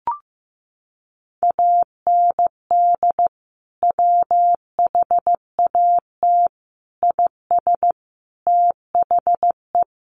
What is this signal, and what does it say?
Morse code, heard as a keyed tone, about 15 wpm, 700 Hz. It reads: AND WHAT IS THE
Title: Sentences from Top 1500 Words 15wpm